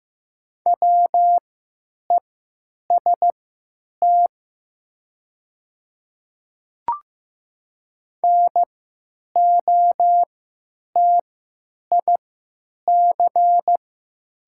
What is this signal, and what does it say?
Morse code, heard as a keyed tone, about 15 wpm, spaced as for 9 wpm, 700 Hz. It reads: WEST NOTIC
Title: 3x Character Spacing - Top 500 Words - 15wpm